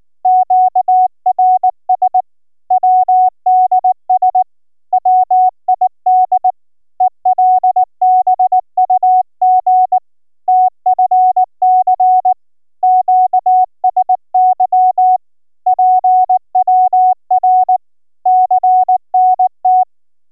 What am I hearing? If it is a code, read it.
Morse code, heard as a keyed tone, about 17 wpm, 740 Hz. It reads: QRS WDS WID ELBUG TFC QSY PWR CNT